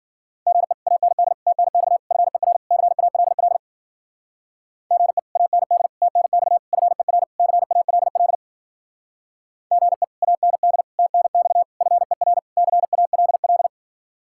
Morse code, heard as a keyed tone, about 51 wpm, 700 Hz. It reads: GE AND TNX FER CALL GE AND TNX FER CALL GE AND TNX FER CALL